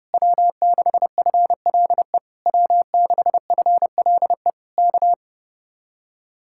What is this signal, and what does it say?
Morse code, heard as a keyed tone, about 30 wpm, 700 Hz. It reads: W6FLE W6FLE K